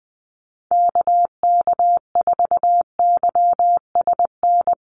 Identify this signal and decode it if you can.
Morse code, heard as a keyed tone, about 19 wpm, 700 Hz. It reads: KK4YSN
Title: USA Call Signs Repeated 2x 20wpm